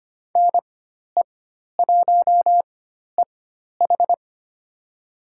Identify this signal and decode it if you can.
Morse code, heard as a keyed tone, about 25 wpm, spaced as for 12 wpm, 700 Hz. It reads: NE1EH
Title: USA Call Signs - 5 Characters - 4x Character Spacing - 25wpm